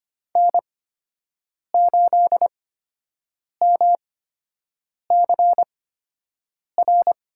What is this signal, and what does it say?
Morse code, heard as a keyed tone, about 25 wpm, spaced as for 7 wpm, 700 Hz. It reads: N8MCR
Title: USA Call Signs - 5 Characters - 8x Character Spacing - 25wpm